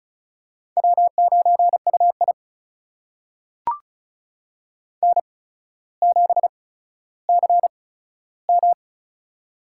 Morse code, heard as a keyed tone, about 36 wpm, spaced as for 15 wpm, 700 Hz. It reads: W9UI N7CM
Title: USA Call Signs - 4 Characters - 8x Character Spacing - 35wpm